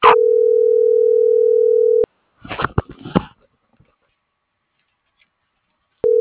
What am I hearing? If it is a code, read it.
Ringback tone (North America), 440 + 480 Hz.